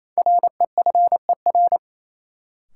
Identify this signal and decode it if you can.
Morse code, heard as a keyed tone, about 28 wpm, 700 Hz. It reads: REFER
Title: Sentences from Top 1000 Words - New Word Drill - 28wpm